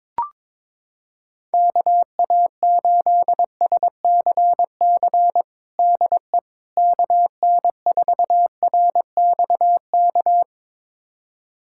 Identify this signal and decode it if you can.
Morse code, heard as a keyed tone, about 22 wpm, 700 Hz. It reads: KA8SCC DE KN4RXK
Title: USA Callsign DE Callsign 22wpm